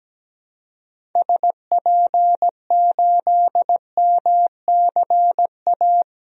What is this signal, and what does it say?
Morse code, heard as a keyed tone, about 17 wpm, 700 Hz. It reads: SP8MCA
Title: DXCC-100 Call Signs - 6 Characters - Repeated 3x - 17wpm